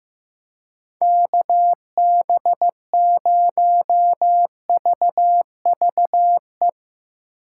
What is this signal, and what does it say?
Morse code, heard as a keyed tone, about 15 wpm, 700 Hz. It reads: KB0VVE
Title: USA Call Signs - 6 Characters - 2x Character Spacing - 15wpm